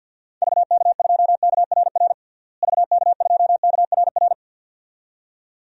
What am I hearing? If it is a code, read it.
Morse code, heard as a keyed tone, about 50 wpm, 700 Hz. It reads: VK2KRR VK2KRR
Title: DXCC-100 Call Signs - 6 Characters - Repeated 2x - 50wpm